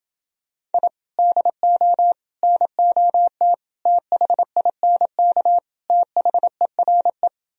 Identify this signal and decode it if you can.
Morse code, heard as a keyed tone, about 30 wpm, 700 Hz. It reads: I DO NOT THINK THERE